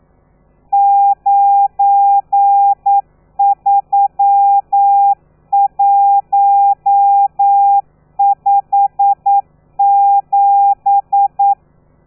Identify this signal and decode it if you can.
Morse code, heard as a keyed tone, about 9 wpm, 800 Hz. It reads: 93157
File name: telegrafia / i2tfj / Lez12 40